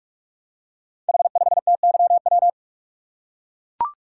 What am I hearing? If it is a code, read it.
Morse code, heard as a keyed tone, about 44 wpm, 690 Hz. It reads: H5TYW
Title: Code Groups - 5 Letter:Numbers - 45wpm